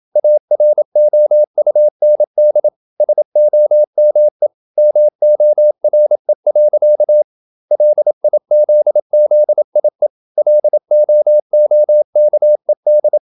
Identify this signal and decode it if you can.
Morse code, heard as a keyed tone, about 27 wpm, 600 Hz. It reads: AROUND SOME MORE. LIZZIE LOOKED
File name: Never 0027wpm